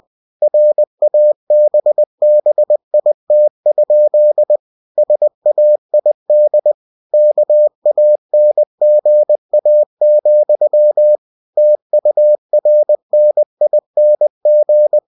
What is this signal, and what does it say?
Morse code, heard as a keyed tone, about 20 wpm, 600 Hz. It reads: RABBIT? SAID KANGA, TURNING
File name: cw-books / winnie-the-pooh / winnie-20wpm-page-0069